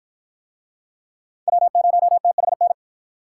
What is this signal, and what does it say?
Morse code, heard as a keyed tone, about 48 wpm, 700 Hz. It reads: W0THN